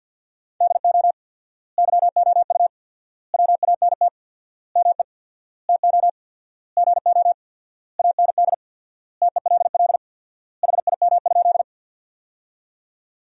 Morse code, heard as a keyed tone, about 47 wpm, 700 Hz. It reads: DO YOU WANT ME TO GO AND TELL HIM?
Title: 4x Word-Spacing - Sentences from Top 200 Words - 50wpm